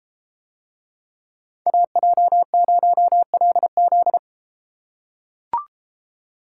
Morse code, heard as a keyed tone, about 33 wpm, 700 Hz. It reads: AJ0LZ